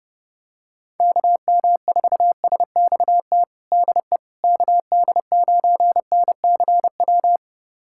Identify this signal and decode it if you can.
Morse code, heard as a keyed tone, about 30 wpm, 700 Hz. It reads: KM4SXT DE KD9NCW
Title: USA Callsign DE Callsign - 6 Characters - 30wpm